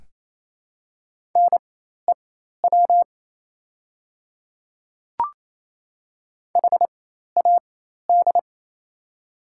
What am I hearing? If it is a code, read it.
Morse code, heard as a keyed tone, about 28 wpm, spaced as for 13 wpm, 700 Hz. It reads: NEW HAD
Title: 4x Character Spacing - Top 200 Words - 28wpm